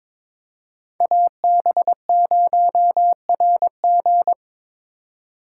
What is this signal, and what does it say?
Morse code, heard as a keyed tone, about 22 wpm, 700 Hz. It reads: AB0RG
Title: USA Call Signs - 3x Character Spacing - 22wpm